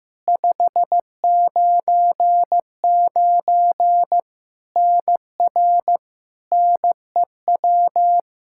Morse code, heard as a keyed tone, about 15 wpm, 700 Hz. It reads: 599 NR NEW
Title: Practice QSO - Exchange 1 - Beginning - 15wpm